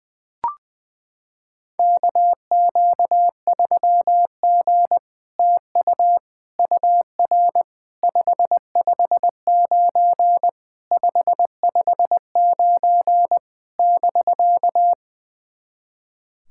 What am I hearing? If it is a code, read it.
Morse code, heard as a keyed tone, about 20 wpm, 700 Hz. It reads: KQ3G TU UR 559 559 <BK>
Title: SOTA QSO Practice 20wpm